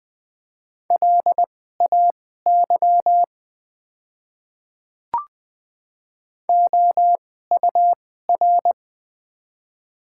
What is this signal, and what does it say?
Morse code, heard as a keyed tone, about 20 wpm, spaced as for 14 wpm, 700 Hz. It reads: LAY OUR
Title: 2x Character Spacing - Top 500 Words - 20wpm